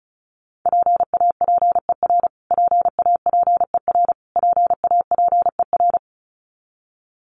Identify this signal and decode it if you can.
Morse code, heard as a keyed tone, about 34 wpm, 700 Hz. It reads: PAPER PAPER PAPER
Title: Top 500 Words Repeated 35wpm